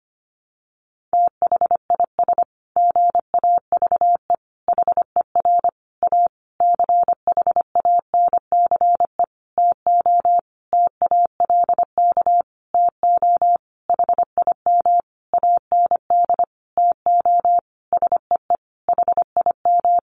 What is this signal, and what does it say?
Morse code, heard as a keyed tone, about 25 wpm, 700 Hz. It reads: THIS GAVE HER A CHANCE TO TALK TO HIM AND TO SEE HIM
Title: Sentences from Top 1400 Words 25wpm